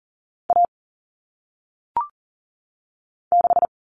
Morse code, heard as a keyed tone, about 42 wpm, 700 Hz. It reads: A 6
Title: All Cut Numbers - 1-Digit 40wpm